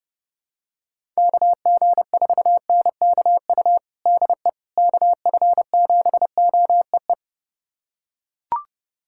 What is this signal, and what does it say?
Morse code, heard as a keyed tone, about 30 wpm, 700 Hz. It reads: KG4NKU DE KF7OEE
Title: USA Callsign DE Callsign - 6 Characters - 30wpm